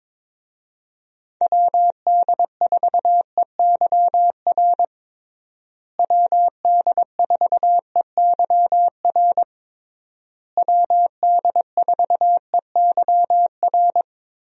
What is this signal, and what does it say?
Morse code, heard as a keyed tone, about 22 wpm, 700 Hz. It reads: WD4EYR WD4EYR WD4EYR
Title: USA Call Signs - 6 Characters - Repeated 3x - 22wpm